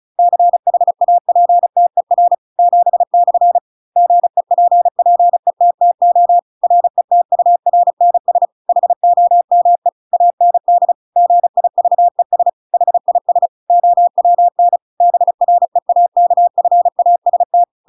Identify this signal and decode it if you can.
Morse code, heard as a keyed tone, about 34 wpm, 700 Hz. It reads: CHAPTER 7/ GEPPETTO RETURNS HOME AND GIVES HIS OWN BREAKFAST
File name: Pinocchio / Pinocchio ch7 35 35 ik1hge